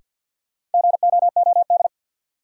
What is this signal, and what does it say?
Morse code, heard as a keyed tone, about 53 wpm, 700 Hz. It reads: GOOD
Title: Sentences from Top 200 Words - New Word Drill - 50wpm